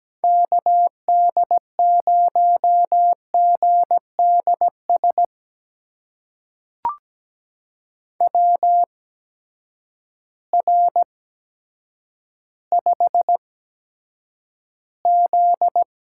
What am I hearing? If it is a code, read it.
Morse code, heard as a keyed tone, about 17 wpm, spaced as for 8 wpm, 700 Hz. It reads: KD0GDS WR5Z
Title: USA Call Signs - 8x Character Spacing - 17wpm